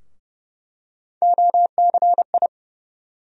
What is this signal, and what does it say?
Morse code, heard as a keyed tone, about 31 wpm, 700 Hz. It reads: OCI
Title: USA Call Sign Postfixes - 3 Letters 30wpm